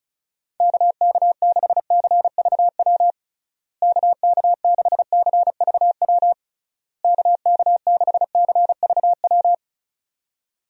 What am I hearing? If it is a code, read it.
Morse code, heard as a keyed tone, about 36 wpm, 700 Hz. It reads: KK6CVW KK6CVW KK6CVW
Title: USA Call Signs - 6 Characters - Repeated 3x - 35wpm